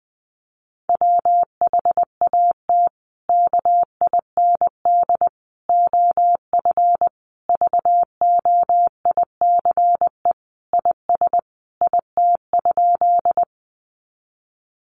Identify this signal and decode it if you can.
Morse code, heard as a keyed tone, about 20 wpm, 700 Hz. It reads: WHAT KIND OF VOICE IS IT?
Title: Sentences from Top 500 Words 20wpm